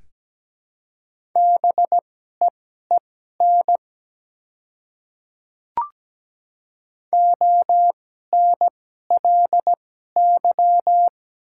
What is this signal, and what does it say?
Morse code, heard as a keyed tone, about 17 wpm, spaced as for 12 wpm, 700 Hz. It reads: BEEN ONLY
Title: 2x Character Spacing - Top 200 Words - 17wpm